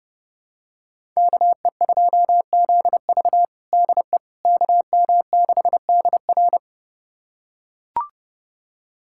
Morse code, heard as a keyed tone, about 30 wpm, 700 Hz. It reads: KE2ZV DE KM6DR
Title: USA Callsign DE Callsign - 5 Characters - 30wpm